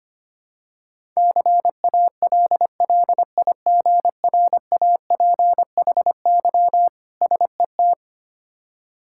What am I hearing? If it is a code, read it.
Morse code, heard as a keyed tone, about 25 wpm, 700 Hz. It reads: CALLIGRAPHY SET
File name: practice / 25wpm / Difficult Words with Hints - 25wpm